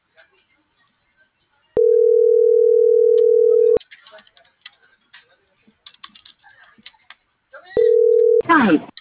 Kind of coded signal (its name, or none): telephone call progress tone